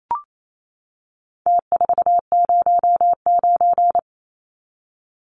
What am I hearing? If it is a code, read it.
Morse code, heard as a keyed tone, about 28 wpm, 700 Hz. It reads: T409
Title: Numbers - Contest Style - 4-Digit 28wpm